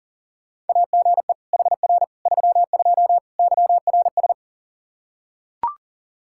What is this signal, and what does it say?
Morse code, heard as a keyed tone, about 40 wpm, 700 Hz. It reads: AGE HR 32 YRS
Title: Practice QSO - Exchange 3 - More Info - 40wpm